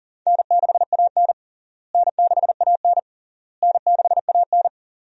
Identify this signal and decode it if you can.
Morse code, heard as a keyed tone, about 40 wpm, 700 Hz. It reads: N6AN N6AN N6AN
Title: USA Call Signs - 4 Characters - Repeated 3x - 40wpm